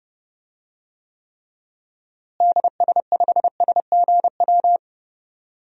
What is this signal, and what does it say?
Morse code, heard as a keyed tone, about 29 wpm, 700 Hz. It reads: DS5SGW
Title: DXCC-100 Call Signs - 6 Characters - Repeated 3x - 30wpm